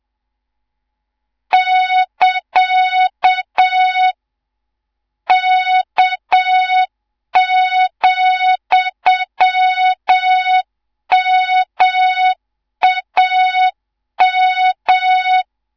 Morse code, heard as a keyed tone, about 7 wpm, 3700 Hz, with noise. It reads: <KA> K,MAM
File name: Morze-HA2MN / hang / 18 IRASJELEK2